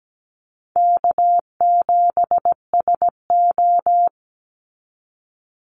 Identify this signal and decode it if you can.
Morse code, heard as a keyed tone, about 17 wpm, 700 Hz. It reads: K7SO
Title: NA SOTA Activator Callsigns 17wpm